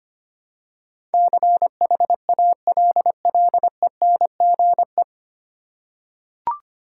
Morse code, heard as a keyed tone, about 25 wpm, 700 Hz. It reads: CHALLENGE
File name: Nine Letter Words - 25wpm